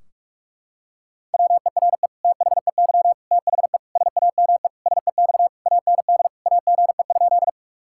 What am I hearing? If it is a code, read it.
Morse code, heard as a keyed tone, about 46 wpm, 700 Hz. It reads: WERE THEY THE SAME SEX AND AGE?
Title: Sentences from Top 1600 Words 45wpm